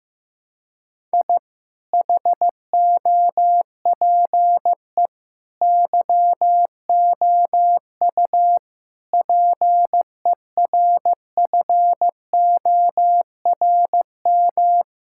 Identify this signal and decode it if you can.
Morse code, heard as a keyed tone, about 15 wpm, 700 Hz. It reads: I HOPE YOU PERFORM
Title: Sentences from Top 1000 Words - Review Part 6 - 15wpm